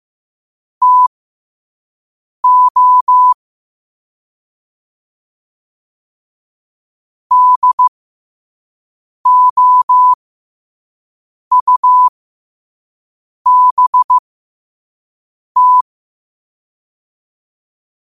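Morse code, heard as a keyed tone, about 15 wpm, spaced as for 5 wpm, 1000 Hz. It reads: TO DOUBT
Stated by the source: Morse code quote of the day sent at 5 wpm.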